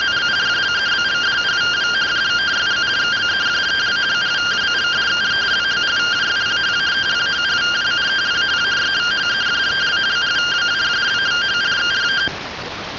Two-tone teleprinter signal, about 170 Hz shift.
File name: findings / CT 205 ax mips, iDose 3 0006.gen-35 rtty rx